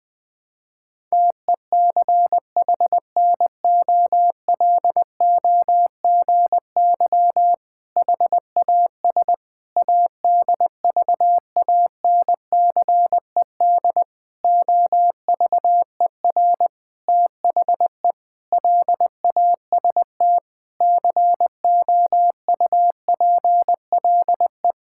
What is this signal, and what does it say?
Morse code, heard as a keyed tone, about 19 wpm, 700 Hz. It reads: TECHNOLOGY HAS ADVANCED OVER THE LAST COUPLE
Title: Sentences with 6-Word Phrases - 20wpm